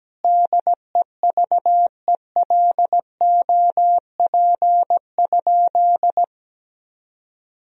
Morse code, heard as a keyed tone, about 17 wpm, 700 Hz. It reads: DEVELOP?
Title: Sentences from Top 1000 Words - Review Part 2 - 17wpm